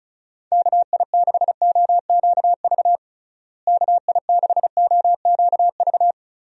Morse code, heard as a keyed tone, about 34 wpm, 700 Hz. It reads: KI6OQV KI6OQV
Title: USA Call Signs - 6 Characters - Repeated 2x - 35wpm